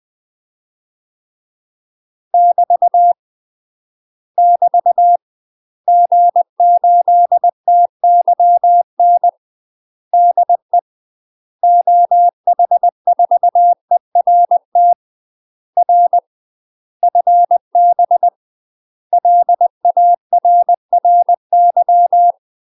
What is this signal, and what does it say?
Morse code, heard as a keyed tone, about 20 wpm, 700 Hz. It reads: <BT> <BT> G8TYN DE OH4ERT R FB LARRY
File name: UK MASTER QSOs TRAINING / 20WPM / UK MASTER QSOs 0048